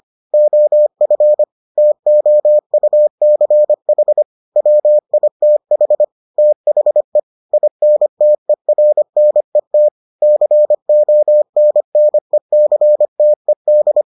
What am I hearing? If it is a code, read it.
Morse code, heard as a keyed tone, about 25 wpm, 600 Hz. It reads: OF TOUCH WITH THE INTERNET CONNECTED